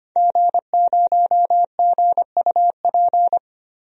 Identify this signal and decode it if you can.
Morse code, heard as a keyed tone, about 25 wpm, 700 Hz. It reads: G0GUP
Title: DXCC-100 Call Signs - 5 Characters - Repeated 3x - 25wpm